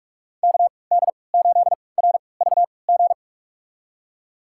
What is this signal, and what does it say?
Morse code, heard as a keyed tone, about 46 wpm, spaced as for 25 wpm, 700 Hz. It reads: KD8RVG
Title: USA Call Signs - 6 Characters - 3x Character Spacing - 45wpm